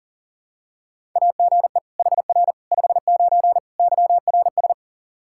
Morse code, heard as a keyed tone, about 37 wpm, 700 Hz. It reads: AGE HR 59 YRS
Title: Practice QSO - Exchange 3 - More Info - 40wpm